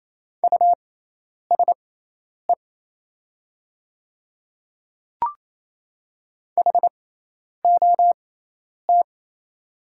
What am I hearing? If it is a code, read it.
Morse code, heard as a keyed tone, about 27 wpm, spaced as for 10 wpm, 700 Hz. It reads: USE HOT
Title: 6x Character Spacing - Top 100 Words - 28wpm